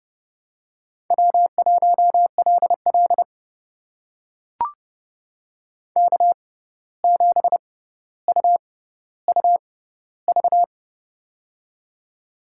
Morse code, heard as a keyed tone, about 30 wpm, spaced as for 14 wpm, 700 Hz. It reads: W1LL K7UUV
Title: USA Call Signs - 6x Character Spacing - 30wpm